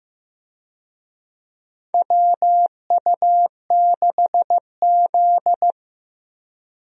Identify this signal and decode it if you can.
Morse code, heard as a keyed tone, about 15 wpm, 700 Hz. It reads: WU6Z
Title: USA Call Signs - 4 Characters - Repeated 2x - 15wpm